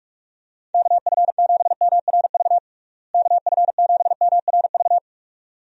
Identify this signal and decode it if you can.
Morse code, heard as a keyed tone, about 42 wpm, 700 Hz. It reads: KF7MRV KF7MRV
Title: USA Call Signs - 6 Characters - Repeated 2x - 45wpm